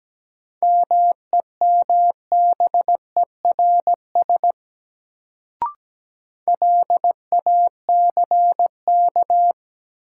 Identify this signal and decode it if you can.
Morse code, heard as a keyed tone, about 17 wpm, 700 Hz. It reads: MEMBERS LACK
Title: Sets of 3 Words - New Word Drill - 17wpm